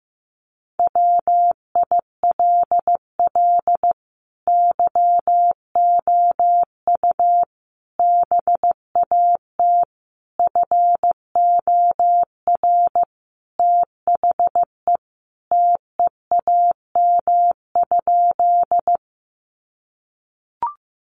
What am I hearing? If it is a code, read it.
Morse code, heard as a keyed tone, about 15 wpm, 700 Hz. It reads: WILL YOU BAT FOR THE TEAM?
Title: Sentences from Top 1500 Words 15wpm